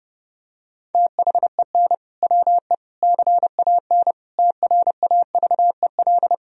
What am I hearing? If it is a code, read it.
Morse code, heard as a keyed tone, about 30 wpm, 700 Hz. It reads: THEN WE CAN TRAVEL
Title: Sentences from Top 1000 Words - Review Part 2 - 30wpm